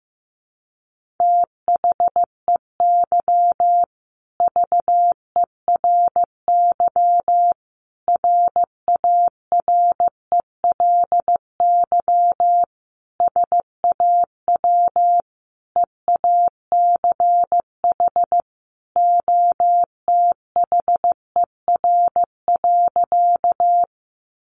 Morse code, heard as a keyed tone, about 15 wpm, 700 Hz. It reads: THEY VERY RARELY SAW EACH OTHER.
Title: Sentences from Top 1400 Words 15wpm